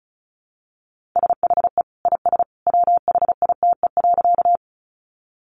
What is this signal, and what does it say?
Morse code, heard as a keyed tone, about 34 wpm, 700 Hz. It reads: SHE IS WHITE.
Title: Easy Phrases 35wpm